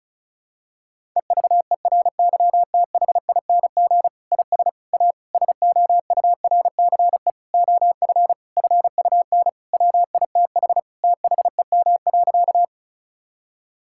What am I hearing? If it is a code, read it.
Morse code, heard as a keyed tone, about 34 wpm, 700 Hz. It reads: EVERYTHING IS A SOURCE OF FUN WITH THEM.
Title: Sentences from Top 1000 Words - Review Part 5 - 35wpm